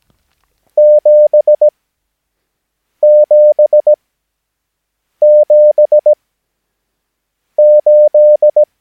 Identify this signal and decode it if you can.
Morse code, heard as a keyed tone, about 17 wpm, spaced as for 6 wpm, 600 Hz. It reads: 777 8
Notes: Övningsmaterial med alfabetet A till Ö samt siffror 0 till 9 enligt International/Continental Morse Code.